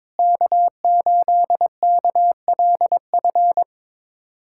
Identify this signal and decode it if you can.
Morse code, heard as a keyed tone, about 22 wpm, 700 Hz. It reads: K8KLF
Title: USA Callsign DE Callsign - 5 Characters - 22wpm